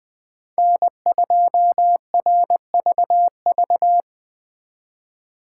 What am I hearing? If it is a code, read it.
Morse code, heard as a keyed tone, about 24 wpm, 700 Hz. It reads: N2RVV
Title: Speed Racing - USA 1x3 Call Signs 20wpm